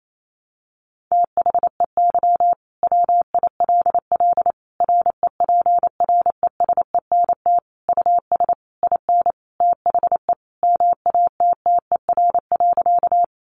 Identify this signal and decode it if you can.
Morse code, heard as a keyed tone, about 28 wpm, 700 Hz. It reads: THEY WILL REPRESENT US IN THE MATTER.
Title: Sentences from Top 1200 Words 28wpm